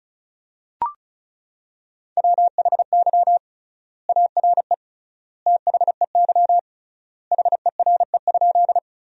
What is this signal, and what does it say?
Morse code, heard as a keyed tone, about 36 wpm, 700 Hz. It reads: WHY ARE THEY HERE?
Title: 3x Word-Spacing - Sentences from Top 200 Words - 35wpm